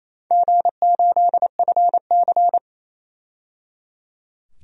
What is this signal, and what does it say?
Morse code, heard as a keyed tone, about 28 wpm, 700 Hz. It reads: G8FC
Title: DXCC-100 Call Signs - 4 Characters - 28wpm